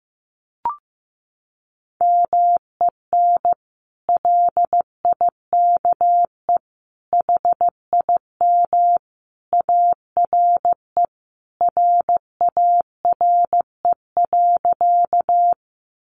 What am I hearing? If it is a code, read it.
Morse code, heard as a keyed tone, about 15 wpm, 700 Hz. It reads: MEN LIKE HIM ARE RARE.
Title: Sentences from Top 1100 Words 15wpm